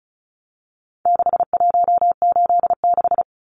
Morse code, heard as a keyed tone, about 34 wpm, 700 Hz. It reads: 6186
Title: Numbers - 4-Digit 35wpm